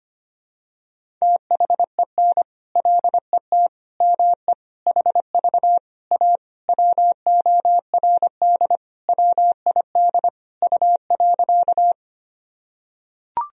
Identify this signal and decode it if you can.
Morse code, heard as a keyed tone, about 25 wpm, 700 Hz. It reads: THEN LET ME HV A WORD WID U.
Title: Sentences from Top 300 Words and Abbreviations 25wpm